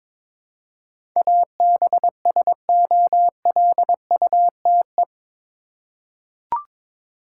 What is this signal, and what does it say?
Morse code, heard as a keyed tone, about 22 wpm, 700 Hz. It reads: ABSOLUTE
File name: practice / 22wpm / Eight Letter Words - 22wpm